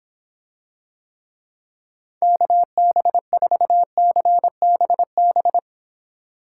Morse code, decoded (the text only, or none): KB4CBB